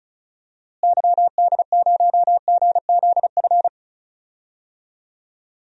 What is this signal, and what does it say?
Morse code, heard as a keyed tone, about 34 wpm, 700 Hz. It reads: YD0GZF
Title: DXCC-100 Call Signs - 6 Characters - Repeated 3x - 35wpm